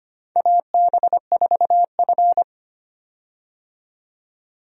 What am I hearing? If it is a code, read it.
Morse code, heard as a keyed tone, about 25 wpm, 700 Hz. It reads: AB4F
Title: USA Call Signs - 4 Characters - 25wpm